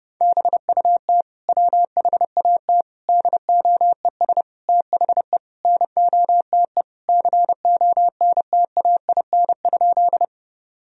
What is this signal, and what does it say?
Morse code, heard as a keyed tone, about 30 wpm, 700 Hz. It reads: BUT WHAT DOES THE NOTE CONTAIN?
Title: Sentences from Top 1000 Words - Review Part 2 - 30wpm